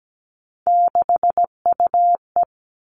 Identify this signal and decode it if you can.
Morse code, heard as a keyed tone, about 17 wpm, 700 Hz. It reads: 6UE
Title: Three Challenging Letter-Number Combinations 17wpm